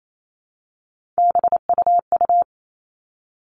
Morse code, heard as a keyed tone, about 28 wpm, 700 Hz. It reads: BUU
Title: All Cut Numbers - 3-Digit 28wpm